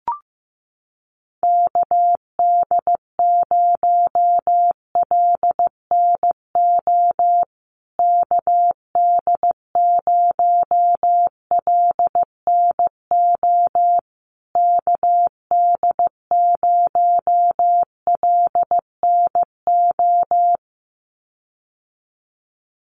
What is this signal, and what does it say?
Morse code, heard as a keyed tone, about 15 wpm, 700 Hz. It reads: KD0LNO KD0LNO KD0LNO
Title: USA Call Signs Repeated 3x 15wpm